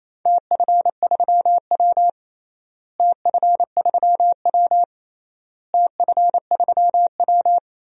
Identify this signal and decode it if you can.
Morse code, heard as a keyed tone, about 28 wpm, 700 Hz. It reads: TF3W TF3W TF3W
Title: DXCC-100 Call Signs - 4 Characters - Repeated 3x - 28wpm